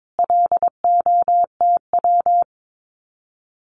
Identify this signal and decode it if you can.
Morse code, heard as a keyed tone, about 22 wpm, 700 Hz. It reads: LOTW
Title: Common QSO Phrases Encore 22wpm